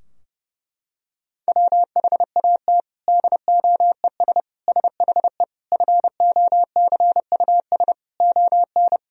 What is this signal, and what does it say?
Morse code, heard as a keyed tone, about 30 wpm, 700 Hz. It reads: WHAT DOES SHE FOCUS ON
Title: Sentences from Top 1000 Words - Review Part 4 - 30wpm